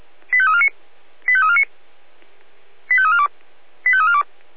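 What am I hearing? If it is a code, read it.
Five-tone selective call (ZVEI), heard as quick runs of five tones, tones of about 70 ms. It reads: 85349 85349 85342 85342
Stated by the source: ZVEI five-tone selective call 85241, received on SDR.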